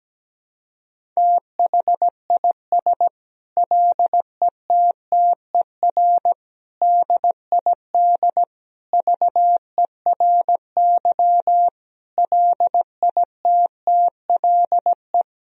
Morse code, heard as a keyed tone, about 17 wpm, 700 Hz. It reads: THIS LETTER DID VERY LITTLE
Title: Sentences from Top 1000 Words - Review Part 4 - 17wpm